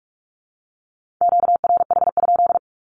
Morse code, heard as a keyed tone, about 44 wpm, 700 Hz. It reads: QRH?
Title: Q-Codes Repeated 3x 45wpm